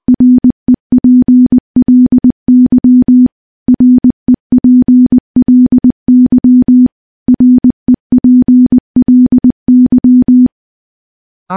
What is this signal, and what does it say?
Morse code, heard as a keyed tone, about 21 wpm, 250 Hz. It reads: REPLY REPLY REPLY
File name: morse / morse-training-20WPM